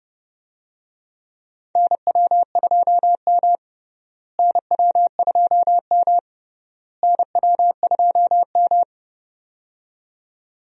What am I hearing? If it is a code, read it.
Morse code, heard as a keyed tone, about 32 wpm, 700 Hz. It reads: NW2M NW2M NW2M
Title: USA Call Signs - 4 Characters - Repeated 3x - 30wpm